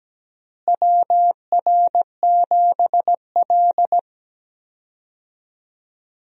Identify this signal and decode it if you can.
Morse code, heard as a keyed tone, about 17 wpm, 700 Hz. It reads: WR7L
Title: USA Call Signs - 4 Characters - Repeated 3x - 17wpm